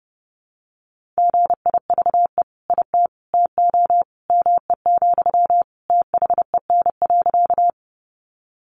Morse code, heard as a keyed tone, about 30 wpm, 700 Hz. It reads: GIVE IT TO ME, THEN.
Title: Sentences from Top 200 Words 30wpm